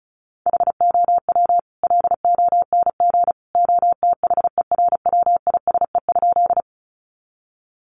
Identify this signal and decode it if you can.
Morse code, heard as a keyed tone, about 34 wpm, 700 Hz. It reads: HOW LONG OTHERWISE?
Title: Sentences from Top 1200 Words 35wpm